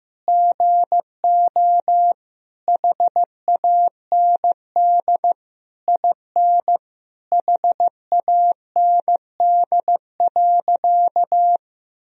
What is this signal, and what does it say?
Morse code, heard as a keyed tone, about 16 wpm, 700 Hz. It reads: GO HAND IN HAND.
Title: Sentences from Top 1000 Words - Review Part 3 - 15wpm